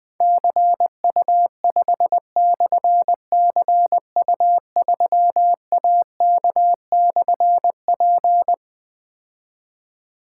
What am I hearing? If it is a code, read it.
Morse code, heard as a keyed tone, about 20 wpm, 700 Hz. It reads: CU5/CU3AK/P
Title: Difficult Call Signs 20wpm